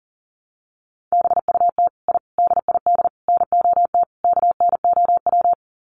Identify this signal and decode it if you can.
Morse code, heard as a keyed tone, about 40 wpm, 700 Hz. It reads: BUT I DID NOT KNOW
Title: Sets of 5 Words Encore 40wpm